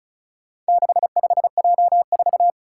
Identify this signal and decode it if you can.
Morse code, heard as a keyed tone, about 36 wpm, 700 Hz. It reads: 65J4
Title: Four Challenging Letter-Number Combinations 35wpm